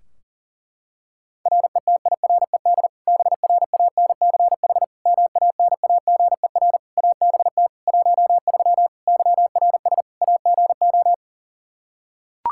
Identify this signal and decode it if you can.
Morse code, heard as a keyed tone, about 37 wpm, 700 Hz. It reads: RETIRED BRANCH MANAGER ABT 13 YRS AGO
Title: K1USN SST QSO Practice 40wpm